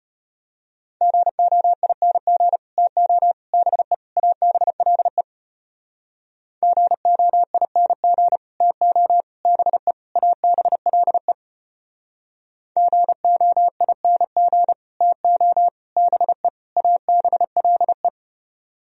Morse code, heard as a keyed tone, about 33 wpm, 700 Hz. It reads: GOING TO BE ABLE GOING TO BE ABLE GOING TO BE ABLE